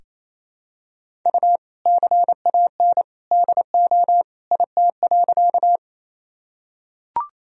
Morse code, heard as a keyed tone, about 28 wpm, 700 Hz. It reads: U CAN DO IT.
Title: Sentences from Top 100 Words and Abbreviations 28wpm